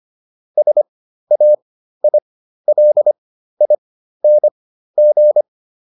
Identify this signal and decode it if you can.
Morse code, heard as a keyed tone, about 25 wpm, spaced as for 13 wpm, 600 Hz. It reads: SAILING